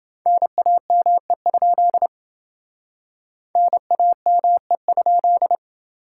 Morse code, heard as a keyed tone, about 29 wpm, 700 Hz. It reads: NAME? NAME?